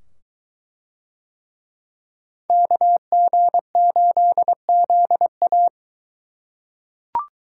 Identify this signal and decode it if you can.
Morse code, heard as a keyed tone, about 23 wpm, 700 Hz. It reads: KG8ZA